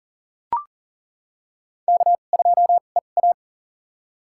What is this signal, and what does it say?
Morse code, heard as a keyed tone, about 40 wpm, spaced as for 29 wpm, 700 Hz. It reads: K2EA
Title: USA Call Signs - 2x Character Spacing - 40wpm